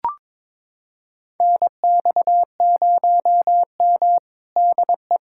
Morse code, heard as a keyed tone, about 22 wpm, 700 Hz. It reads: NX0M DE
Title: USA Callsign DE Callsign - 4 Characters - 22wpm